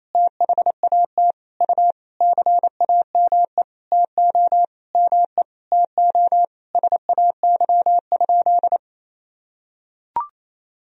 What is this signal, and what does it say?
Morse code, heard as a keyed tone, about 28 wpm, 700 Hz. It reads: THAT U CAME TO ME TO SAY?
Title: Sentences from Top 200 Words and Abbreviations 28wpm